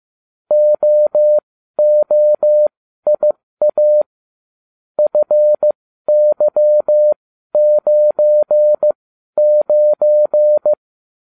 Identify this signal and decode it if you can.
Morse code, heard as a keyed tone, about 15 wpm, spaced as for 12 wpm, 600 Hz. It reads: OOIA FY99